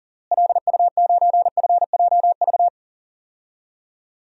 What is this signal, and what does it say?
Morse code, heard as a keyed tone, about 40 wpm, 700 Hz. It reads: LU9FJV
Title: DXCC-100 Call Signs - 6 Characters - Repeated 3x - 40wpm